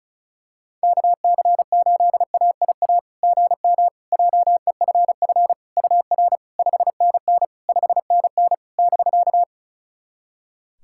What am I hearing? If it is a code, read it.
Morse code, heard as a keyed tone, about 36 wpm, 690 Hz. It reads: KC8AIA GM JEFF UR 5NN 5NN <BK>
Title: SOTA QSO Practice 35wpm